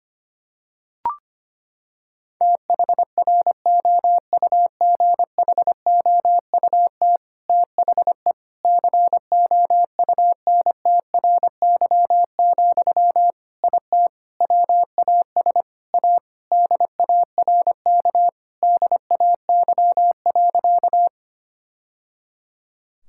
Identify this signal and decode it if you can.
Morse code, heard as a keyed tone, about 25 wpm, 700 Hz. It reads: THROUGHOUT THE COUNTRY, IT WAS A DARK DAY.
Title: Sentences from Top 1000 Words - Review Part 5 - 25wpm